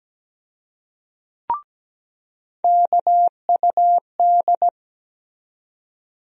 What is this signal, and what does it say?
Morse code, heard as a keyed tone, about 17 wpm, 700 Hz. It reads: KUD
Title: USA Call Sign Postfixes - 3 Letters 17wpm